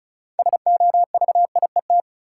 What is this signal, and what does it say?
Morse code, heard as a keyed tone, about 36 wpm, 700 Hz. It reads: SOVIET
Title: Sets of 5 Words Encore - New Word Drill - 35wpm